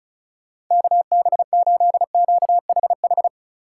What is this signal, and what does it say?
Morse code, heard as a keyed tone, about 46 wpm, 700 Hz. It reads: KB8QHH